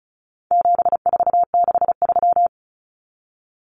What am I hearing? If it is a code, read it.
Morse code, heard as a keyed tone, about 34 wpm, 700 Hz. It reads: 7463
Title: Common Cut Numbers - 4-Digit 35wpm